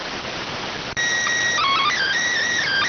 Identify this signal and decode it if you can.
SSTV (slow-scan television), PD 290.